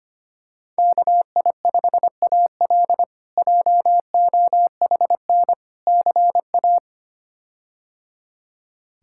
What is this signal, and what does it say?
Morse code, heard as a keyed tone, about 25 wpm, 700 Hz. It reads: KI5AL JOHN CA
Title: K1USN SST QSO Practice - 25wpm